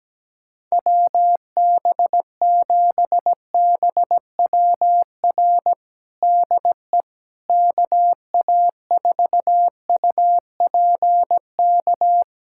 Morse code, heard as a keyed tone, about 17 wpm, 700 Hz. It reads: WB7BWR DE KA4UPK
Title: USA Callsign DE Callsign 17wpm